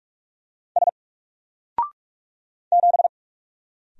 Morse code, heard as a keyed tone, about 44 wpm, 700 Hz. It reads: S 7
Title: Single Character (Letter:Number:Punctuation:Prosign) 45wpm